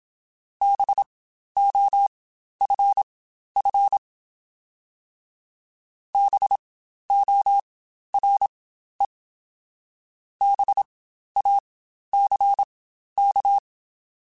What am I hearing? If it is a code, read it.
Morse code, heard as a keyed tone, about 27 wpm, spaced as for 12 wpm, 800 Hz. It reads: BOFF BORE BACK
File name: Spaced Alphas / Spaced Alphas mp3s / Bs sp @25